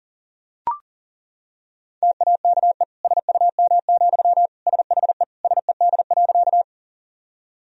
Morse code, heard as a keyed tone, about 40 wpm, 700 Hz. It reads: TAKE SUM, SHE SED.
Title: Sentences from Top 200 Words and Abbreviations 40wpm